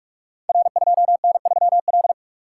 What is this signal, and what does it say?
Morse code, heard as a keyed tone, about 44 wpm, 700 Hz. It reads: R2N3L
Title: Code Groups - 5 Letter:Numbers - 45wpm